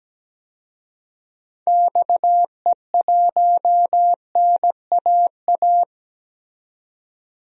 Morse code, heard as a keyed tone, about 17 wpm, 700 Hz. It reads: XE1NAA
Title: DXCC-100 Call Signs - 6 Characters - 17wpm